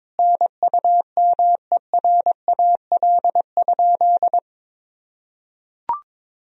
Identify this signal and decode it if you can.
Morse code, heard as a keyed tone, about 22 wpm, 700 Hz. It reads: NUMERAL?
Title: Sentences from Top 1000 Words - Review Part 2 - 22wpm